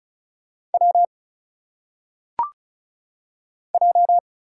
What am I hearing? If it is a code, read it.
Morse code, heard as a keyed tone, about 35 wpm, 700 Hz. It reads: W J
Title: Single Letters - Phonetic Pronunciation 35wpm